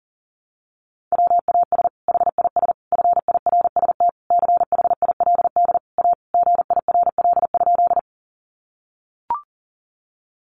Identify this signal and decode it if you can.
Morse code, heard as a keyed tone, about 40 wpm, 700 Hz. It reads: WAS HIS FIRST CHILD A GIRL?
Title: Sentences from Top 800 Words 40wpm